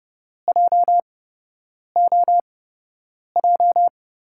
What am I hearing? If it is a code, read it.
Morse code, heard as a keyed tone, about 31 wpm, spaced as for 8 wpm, 700 Hz. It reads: JOJ
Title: USA Call Signs - 8x Character Spacing - 30wpm